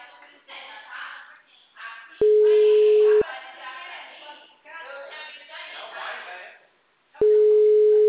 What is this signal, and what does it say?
Ringback tone (Europe, 425 Hz), with speech.
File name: out- -2007- - - .156